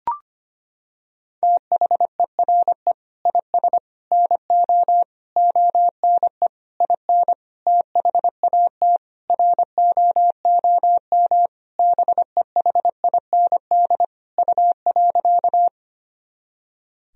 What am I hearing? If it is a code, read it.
Morse code, heard as a keyed tone, about 25 wpm, 700 Hz. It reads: THERE IS NO ONE IN THAT ROOM BEHIND U.
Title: Sentences from Top 500 Words and Abbreviations 25wpm